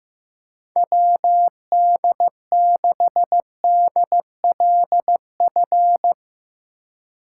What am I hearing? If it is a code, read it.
Morse code, heard as a keyed tone, about 15 wpm, 700 Hz. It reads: WD6DLF
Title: USA Call Signs - 6 Characters - 4x Character Spacing - 15wpm